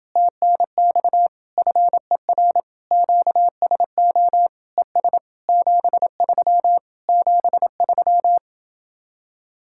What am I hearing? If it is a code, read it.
Morse code, heard as a keyed tone, about 30 wpm, 700 Hz. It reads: TNX FER QSO ES 73 73